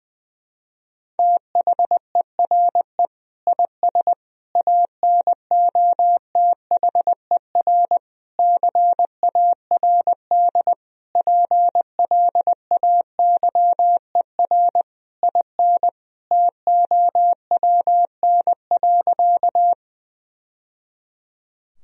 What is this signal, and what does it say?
Morse code, heard as a keyed tone, about 20 wpm, 700 Hz. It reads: THERE IS ANOTHER CARD PLAYER IN TOWN.
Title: Sentences from Top 1000 Words - Review Part 4 - 20wpm